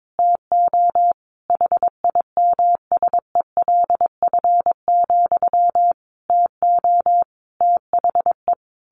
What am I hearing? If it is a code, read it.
Morse code, heard as a keyed tone, about 22 wpm, 700 Hz. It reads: TO HIMSELF, TO THE
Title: A Sound Of Thunder 22wpm